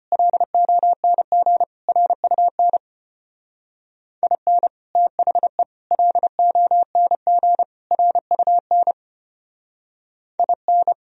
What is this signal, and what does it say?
Morse code, heard as a keyed tone, about 31 wpm, 700 Hz. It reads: LONG RUN IN THE LONG RUN IN
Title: Speed-Racing - Sets of 4 Words 25wpm